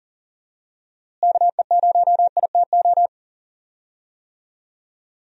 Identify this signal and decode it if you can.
Morse code, heard as a keyed tone, about 40 wpm, 700 Hz. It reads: KE0ITO
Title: USA Call Signs - 6 Characters - Repeated 2x - 40wpm